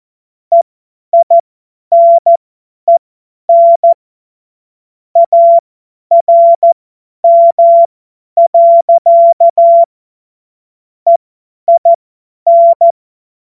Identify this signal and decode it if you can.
Morse code, heard as a keyed tone, about 14 wpm, spaced as for 10 wpm, 680 Hz. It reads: EINEN ARM. EIN
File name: Morsekurs / Witze / Witz 01. August 2023 50 F70